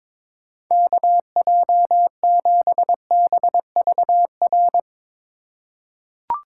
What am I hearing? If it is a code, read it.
Morse code, heard as a keyed tone, about 22 wpm, 700 Hz. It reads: KJ7BVR
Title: USA Call Signs - 6 Characters - 4x Character Spacing - 22wpm